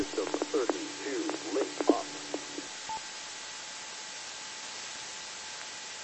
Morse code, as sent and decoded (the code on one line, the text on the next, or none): none